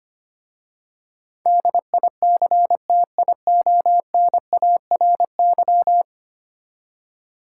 Morse code, heard as a keyed tone, about 25 wpm, 690 Hz. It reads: DICTIONARY
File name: practice / 25wpm / Easy Words with Hints - 25wpm